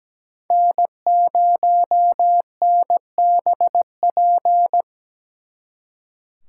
Morse code, heard as a keyed tone, about 20 wpm, 700 Hz. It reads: N0NBP
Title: Speed Racing - USA 1x3 Call Signs 17wpm